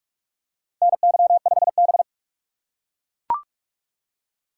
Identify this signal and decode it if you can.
Morse code, heard as a keyed tone, about 42 wpm, 700 Hz. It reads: NY5B
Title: USA Call Signs - 4 Characters - 2x Character Spacing - 45wpm